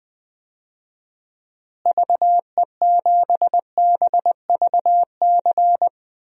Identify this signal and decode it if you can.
Morse code, heard as a keyed tone, about 20 wpm, 700 Hz. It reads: VE7BVC
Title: DXCC-100 Call Signs - 6 Characters - Repeated 2x - 20wpm